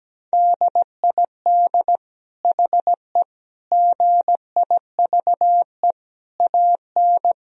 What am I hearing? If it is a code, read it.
Morse code, heard as a keyed tone, about 17 wpm, 700 Hz. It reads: DID HE GIVE AN
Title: Sentences from Top 1000 Words - Review Part 3 - 17wpm